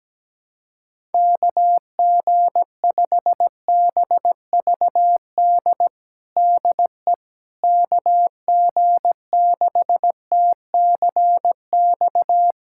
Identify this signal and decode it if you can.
Morse code, heard as a keyed tone, about 17 wpm, 700 Hz. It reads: KG5BVD DE KG6TCX
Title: USA Callsign DE Callsign 17wpm